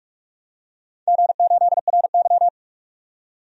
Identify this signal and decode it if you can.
Morse code, heard as a keyed tone, about 44 wpm, 700 Hz. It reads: G8RY
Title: DXCC-100 Call Signs - 4 Characters - 45wpm